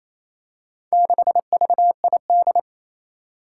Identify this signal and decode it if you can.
Morse code, heard as a keyed tone, about 28 wpm, 700 Hz. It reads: 6VID
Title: Four Challenging Letter-Number Combinations 28wpm